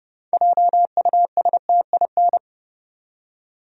Morse code, heard as a keyed tone, about 30 wpm, 700 Hz. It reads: JUSTIN
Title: USA Common Names - 30wpm